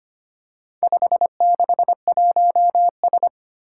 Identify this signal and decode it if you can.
Morse code, heard as a keyed tone, about 25 wpm, 700 Hz. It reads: 561S
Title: Four Challenging Letter-Number Combinations 25wpm